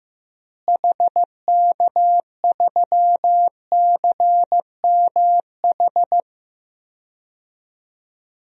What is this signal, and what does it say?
Morse code, heard as a keyed tone, about 15 wpm, 700 Hz. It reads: HK3CMH
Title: DXCC-100 Call Signs - 6 Characters - Repeated 3x - 15wpm